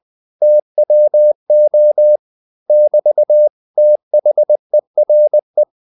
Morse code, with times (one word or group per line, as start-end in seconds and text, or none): 0.4-2.2 TWO
2.7-3.5 <BT>
3.8-5.6 THERE